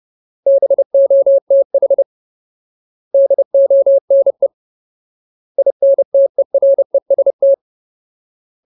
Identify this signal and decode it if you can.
Morse code, heard as a keyed tone, about 30 wpm, 550 Hz. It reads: BOTH DONE INTEREST
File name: Ultimate Word List / 30WPM Ultimate